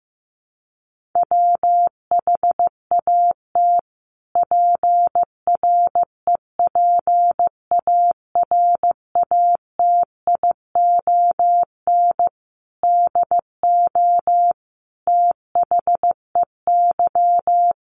Morse code, heard as a keyed tone, about 15 wpm, 700 Hz. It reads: WHAT PREPARATION DO THEY
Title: Sentences from Top 1400 Words 15wpm